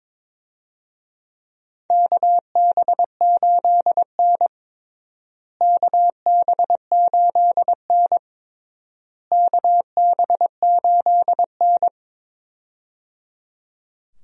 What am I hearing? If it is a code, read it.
Morse code, heard as a keyed tone, about 22 wpm, 700 Hz. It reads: KB8N KB8N KB8N
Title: USA Call Signs - 4 Characters - Repeated 3x - 22wpm